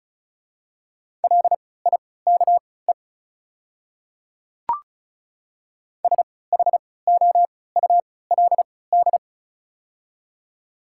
Morse code, heard as a keyed tone, about 34 wpm, spaced as for 20 wpm, 700 Hz. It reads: LIKE SHOULD
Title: 3x Character Spacing - Top 300 Words - 35wpm